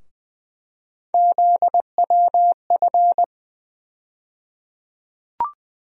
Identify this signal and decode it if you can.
Morse code, heard as a keyed tone, about 21 wpm, 700 Hz. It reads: ZWF
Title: USA Call Sign Postfixes - 3 Letters 20wpm